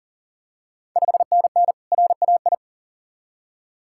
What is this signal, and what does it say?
Morse code, heard as a keyed tone, about 40 wpm, 690 Hz. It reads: 5NN RAI
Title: Ontario QSO Party Exchange 40wpm